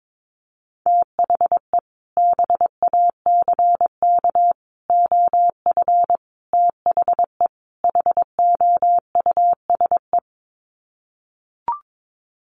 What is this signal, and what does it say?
Morse code, heard as a keyed tone, about 22 wpm, 700 Hz. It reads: THE BACK OF THE HOUSE
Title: Sets of 5 Words Encore2 22wpm